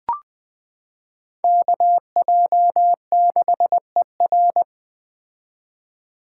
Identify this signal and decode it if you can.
Morse code, heard as a keyed tone, about 20 wpm, 700 Hz. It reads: KJ6ER
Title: 2023 POTA Activator Call Signs - 20wpm